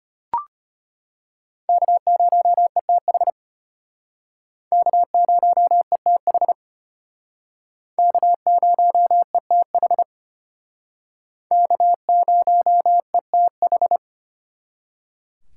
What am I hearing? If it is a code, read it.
Morse code, heard as a keyed tone, about 32 wpm, 700 Hz. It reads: K0ETH K0ETH K0ETH K0ETH